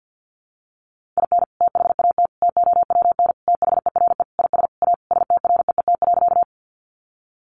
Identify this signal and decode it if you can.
Morse code, heard as a keyed tone, about 50 wpm, 700 Hz. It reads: IN THAT TOWN THERE IS A STREET.
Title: Sentences from Top 500 Words 50wpm